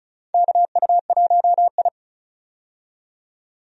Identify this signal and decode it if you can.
Morse code, heard as a keyed tone, about 36 wpm, 700 Hz. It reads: KU1I
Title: USA Call Signs - 4 Characters - 35wpm